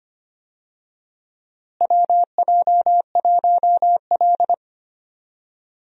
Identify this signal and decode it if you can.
Morse code, heard as a keyed tone, about 25 wpm, 700 Hz. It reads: WJ1L
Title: USA Call Signs - 4 Characters - Repeated 2x - 25wpm